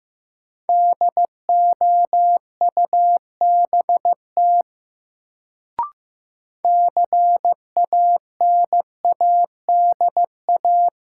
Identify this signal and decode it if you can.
Morse code, heard as a keyed tone, about 16 wpm, 700 Hz. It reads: DOUBT CANADA
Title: Sets of 5 Words - New Word Drill - 15wpm